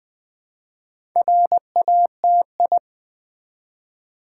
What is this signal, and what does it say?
Morse code, heard as a keyed tone, about 20 wpm, 700 Hz. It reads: RATI
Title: Build A Word - 1 Letter At A Time - 5-Letter Words - 20wpm